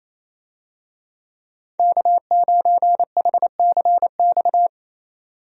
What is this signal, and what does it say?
Morse code, heard as a keyed tone, about 28 wpm, 700 Hz. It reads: K9HCX
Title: USA Call Signs - 5 Characters - 28wpm